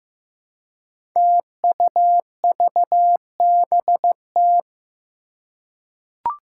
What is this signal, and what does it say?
Morse code, heard as a keyed tone, about 15 wpm, 700 Hz. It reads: TUVBT
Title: Five Challenging Letter-Number Combinations 15wpm